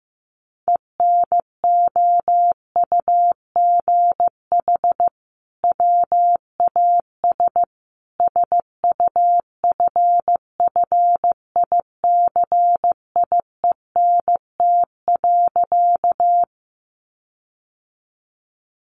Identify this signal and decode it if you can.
Morse code, heard as a keyed tone, about 15 wpm, 700 Hz. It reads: ENOUGH WAS SUFFICIENT.
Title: Sentences from Top 1400 Words 15wpm